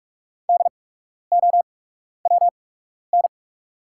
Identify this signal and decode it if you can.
Morse code, heard as a keyed tone, about 48 wpm, spaced as for 12 wpm, 700 Hz. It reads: DOWN
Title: 8x Character Spacing - Top 100 Words - 45wpm